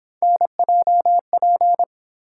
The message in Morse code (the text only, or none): NJP